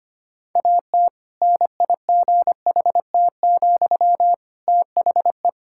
Morse code, heard as a keyed tone, about 25 wpm, 700 Hz. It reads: AT NIGHT, THE
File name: practice / 25wpm / Difficult Sentences with Hints - 25wpm